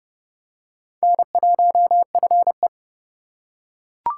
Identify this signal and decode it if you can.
Morse code, heard as a keyed tone, about 30 wpm, 700 Hz. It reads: N1FE
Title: K1USN SST QSO Practice - 30wpm